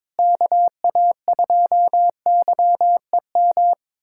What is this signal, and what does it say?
Morse code, heard as a keyed tone, about 22 wpm, 700 Hz. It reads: KA2YEM
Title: USA Callsign DE Callsign - 6 Characters - 22wpm